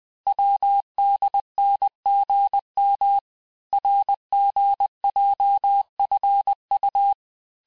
Morse code, heard as a keyed tone, about 20 wpm, 800 Hz. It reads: WDNGM RGJFU
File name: cours-1584-teste-chiffre-vitesse-19mpm